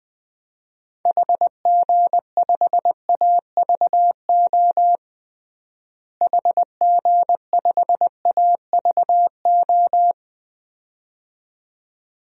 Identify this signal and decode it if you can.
Morse code, heard as a keyed tone, about 20 wpm, 700 Hz. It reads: HG5AVO HG5AVO
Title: DXCC-100 Call Signs - 6 Characters - Repeated 2x - 20wpm